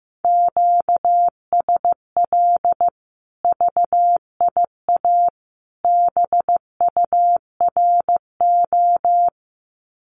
Morse code, heard as a keyed tone, about 15 wpm, 700 Hz. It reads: QSL VIA BURO
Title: Common QSO Phrases 15wpm